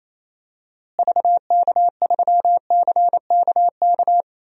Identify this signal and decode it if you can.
Morse code, heard as a keyed tone, about 28 wpm, 700 Hz. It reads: VK3CKK
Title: DXCC-100 Call Signs - 6 Characters - Repeated 3x - 28wpm